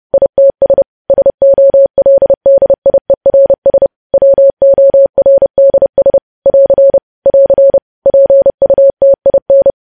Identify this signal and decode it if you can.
Morse code, heard as a keyed tone, about 30 wpm, 550 Hz. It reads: ITS SOLDIERS WORDS <AR> <AR> PUTIN
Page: news headlines in morse code at 30 wpm